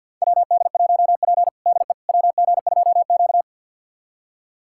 Morse code, heard as a keyed tone, about 46 wpm, 700 Hz. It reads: WD1P DE WG2Q
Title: USA Callsign DE Callsign - 4 Characters - 50wpm